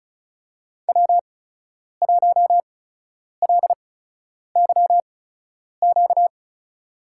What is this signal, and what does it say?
Morse code, heard as a keyed tone, about 36 wpm, spaced as for 10 wpm, 700 Hz. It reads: W1LYQ
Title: USA Call Signs - 8x Character Spacing - 35wpm